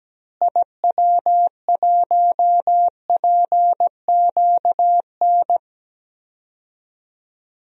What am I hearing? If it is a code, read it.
Morse code, heard as a keyed tone, about 17 wpm, 700 Hz. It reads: IW1PQN
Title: DXCC-100 Call Signs - 6 Characters - Repeated 2x - 17wpm